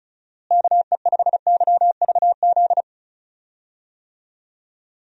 Morse code, heard as a keyed tone, about 36 wpm, 700 Hz. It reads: KE5YVZ
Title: USA Call Signs - 6 Characters - Repeated 3x - 35wpm